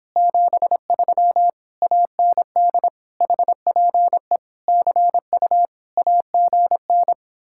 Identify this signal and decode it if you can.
Morse code, heard as a keyed tone, about 26 wpm, 700 Hz. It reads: 73 AND HPE CU AGN